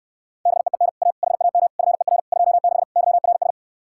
Morse code, heard as a keyed tone, about 68 wpm, 700 Hz. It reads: BEEN A HAM FER 26 YRS